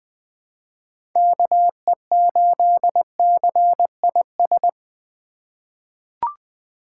Morse code, heard as a keyed tone, about 21 wpm, 700 Hz. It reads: KE8CIS
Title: USA Call Signs - 6 Characters - 2x Character Spacing - 20wpm